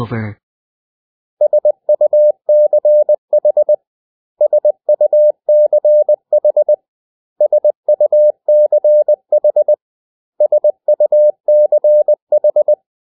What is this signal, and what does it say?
Morse code, heard as a keyed tone, about 20 wpm, 600 Hz. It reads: SUCH SUCH SUCH SUCH
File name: cw / 100 Common English Words